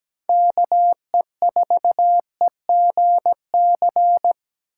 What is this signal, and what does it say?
Morse code, heard as a keyed tone, about 20 wpm, 700 Hz. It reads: KE4EGC
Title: Speed Racing - USA 2x3 Call Signs 17wpm